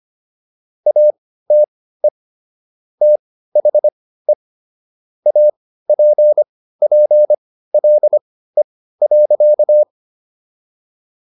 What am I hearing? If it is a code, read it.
Morse code, heard as a keyed tone, about 25 wpm, spaced as for 15 wpm, 600 Hz. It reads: ATE THE APPLE.